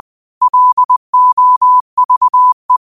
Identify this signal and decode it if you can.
Morse code, heard as a keyed tone, about 20 wpm, 1000 Hz. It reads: LOVE